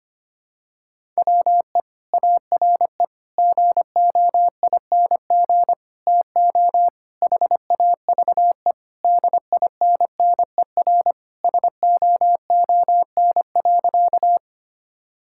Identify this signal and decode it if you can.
Morse code, heard as a keyed tone, about 25 wpm, 700 Hz. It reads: WE ARE GOING TO HAVE DINNER SOON.
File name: practice / 25wpm / Sentences with 3-Word Phrases - 25wpm